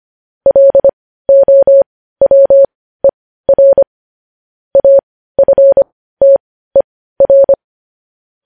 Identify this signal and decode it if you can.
Morse code, heard as a keyed tone, about 25 wpm, spaced as for 15 wpm, 550 Hz. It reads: LOWER AFTER
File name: podcast / News headlines-1x-25x15